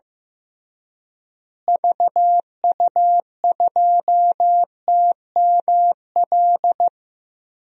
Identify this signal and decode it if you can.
Morse code, heard as a keyed tone, about 15 wpm, 700 Hz. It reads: VU2TML
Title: DXCC-100 Call Signs - 6 Characters - Repeated 2x - 15wpm